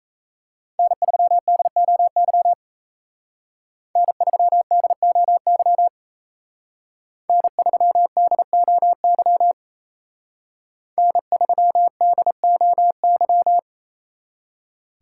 Morse code, heard as a keyed tone, about 33 wpm, 700 Hz. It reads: N3DOY N3DOY N3DOY N3DOY
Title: Speed Racing - USA Call Signs 28wpm